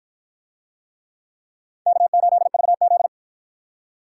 Morse code, heard as a keyed tone, about 51 wpm, 700 Hz. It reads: K8VZ